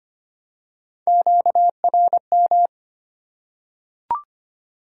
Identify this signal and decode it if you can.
Morse code, heard as a keyed tone, about 25 wpm, 700 Hz. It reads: QRM
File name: Top 90 QSO Elements - 3x Character Spacing - 25wpm